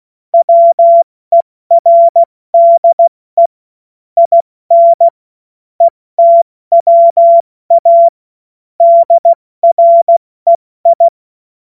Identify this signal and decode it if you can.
Morse code, heard as a keyed tone, about 16 wpm, spaced as for 14 wpm, 680 Hz. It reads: WERDE IN ETWA DREI
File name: Morsekurs / Witze / Witz 30. August 2022 70 F80